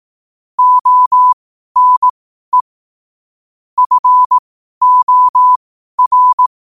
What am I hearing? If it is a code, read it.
Morse code, heard as a keyed tone, about 18 wpm, spaced as for 12 wpm, 1000 Hz. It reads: ONE FOR
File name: qotdfiles / 12wpm